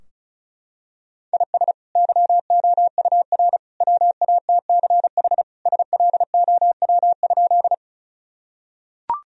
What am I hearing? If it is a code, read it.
Morse code, heard as a keyed tone, about 36 wpm, 700 Hz. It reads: IS YOUR WATCH SLOW?
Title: Sentences from Top 1000 Words - Review Part 2 - 35wpm